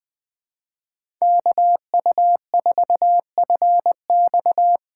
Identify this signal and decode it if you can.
Morse code, heard as a keyed tone, about 20 wpm, 700 Hz. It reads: KU4FX
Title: USA Call Signs - 4x Character Spacing - 20wpm